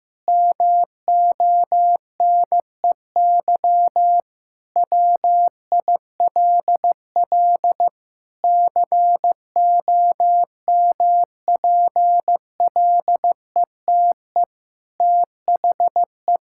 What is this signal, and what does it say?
Morse code, heard as a keyed tone, about 15 wpm, 700 Hz. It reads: MONEY WILL COMPLETE THE
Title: Sentences from Top 1000 Words - Review Part 5 - 15wpm